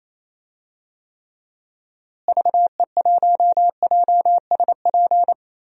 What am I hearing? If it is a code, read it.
Morse code, heard as a keyed tone, about 28 wpm, 700 Hz. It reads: VE1JSP
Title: DXCC-100 Call Signs - 6 Characters - Repeated 2x - 28wpm